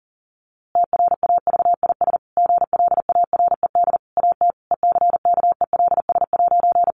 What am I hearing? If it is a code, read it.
Morse code, heard as a keyed tone, about 40 wpm, 700 Hz. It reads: TRAVIS GLARED AT ECKELS'
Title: A Sound Of Thunder 40wpm